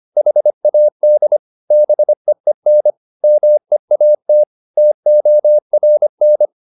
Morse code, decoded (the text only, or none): HAD BEEN MEAT TORN